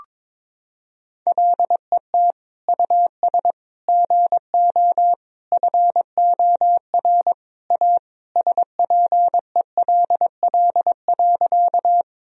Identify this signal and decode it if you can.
Morse code, heard as a keyed tone, about 22 wpm, 700 Hz. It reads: LET US GO FOR A SPELL.
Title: Sentences from Top 1000 Words - Review Part 1 - 22wpm